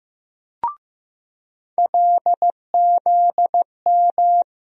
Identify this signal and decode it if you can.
Morse code, heard as a keyed tone, about 15 wpm, 700 Hz. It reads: LZM
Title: USA Call Sign Postfixes - 3 Letters 15wpm